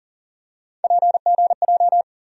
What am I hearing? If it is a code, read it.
Morse code, heard as a keyed tone, about 40 wpm, 700 Hz. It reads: PGJ
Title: USA Call Sign Postfixes - 3 Letters 40wpm